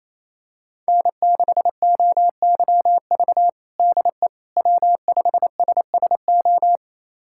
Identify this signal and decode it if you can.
Morse code, heard as a keyed tone, about 28 wpm, 700 Hz. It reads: N6OYV DE W5SSO
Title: USA Callsign DE Callsign - 5 Characters - 28wpm